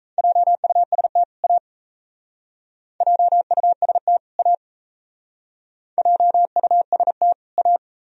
Morse code, heard as a keyed tone, about 40 wpm, 690 Hz. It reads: JUST A JUST A JUST A